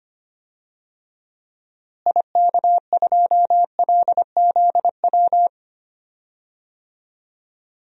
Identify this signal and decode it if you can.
Morse code, heard as a keyed tone, about 25 wpm, 700 Hz. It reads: IK2LZW
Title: DXCC-100 Call Signs - 6 Characters - Repeated 2x - 25wpm